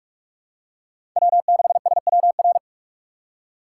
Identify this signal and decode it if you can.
Morse code, heard as a keyed tone, about 46 wpm, 700 Hz. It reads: W6SWR
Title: USA Call Signs - 5 Characters - 3x Character Spacing - 45wpm